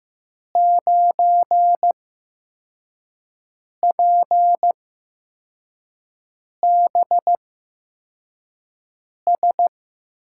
Morse code, heard as a keyed tone, about 15 wpm, spaced as for 4 wpm, 700 Hz. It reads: 9PBS
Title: USA Call Signs - 8x Character Spacing - 15wpm